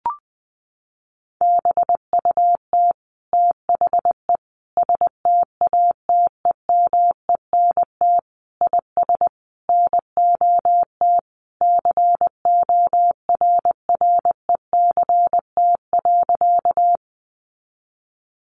Morse code, heard as a keyed tone, about 20 wpm, 700 Hz. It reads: BUT THE STATEMENT IS NOT CORRECT.
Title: Sentences from Top 1200 Words 20wpm